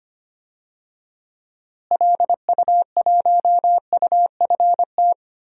Morse code, heard as a keyed tone, about 25 wpm, 700 Hz. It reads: LU1UFT
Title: DXCC-100 Call Signs - 6 Characters - Repeated 2x - 25wpm